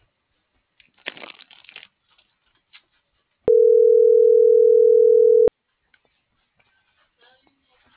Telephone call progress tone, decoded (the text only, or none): ringback tone